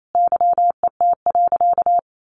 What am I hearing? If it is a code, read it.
Morse code, heard as a keyed tone, about 28 wpm, 700 Hz. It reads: YET.
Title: A Sound Of Thunder 28wpm